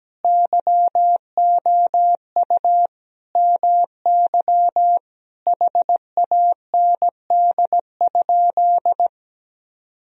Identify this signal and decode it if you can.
Morse code, heard as a keyed tone, about 17 wpm, 700 Hz. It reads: YOU MY HAND?
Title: Sentences from Top 1000 Words - Review Part 4 - 17wpm